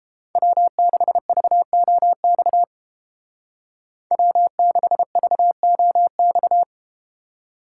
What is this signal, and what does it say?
Morse code, heard as a keyed tone, about 30 wpm, 700 Hz. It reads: W6VOX W6VOX